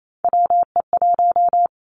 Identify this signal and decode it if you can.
Morse code, heard as a keyed tone, about 28 wpm, 700 Hz. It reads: WE1
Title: Three Challenging Letter-Number Combinations 28wpm